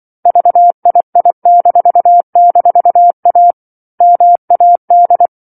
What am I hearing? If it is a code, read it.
Morse code, heard as a keyed tone, about 24 wpm, 700 Hz. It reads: VII--A MAD
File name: 07-Chapter-07 24 WPM